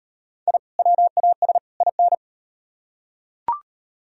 Morse code, heard as a keyed tone, about 35 wpm, 700 Hz. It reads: I WAS IN